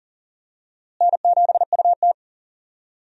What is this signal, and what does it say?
Morse code, heard as a keyed tone, about 40 wpm, 700 Hz. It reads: N7UT
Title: USA Call Signs - 4 Characters - Repeated 2x - 40wpm